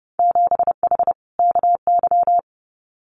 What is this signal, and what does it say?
Morse code, heard as a keyed tone, about 30 wpm, 700 Hz. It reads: 7H KY
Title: Winter Field Day Exchanges Encore 30wpm